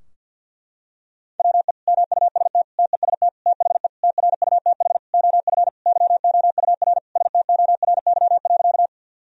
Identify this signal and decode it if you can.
Morse code, heard as a keyed tone, about 46 wpm, 700 Hz. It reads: WE MUST TEST THE TRUTH OF YOUR STORY.
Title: Sentences from Top 1000 Words - Review Part 3 - 50wpm